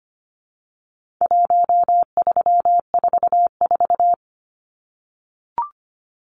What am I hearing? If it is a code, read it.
Morse code, heard as a keyed tone, about 25 wpm, 700 Hz. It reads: 1344
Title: Numbers - 4-Digit 25wpm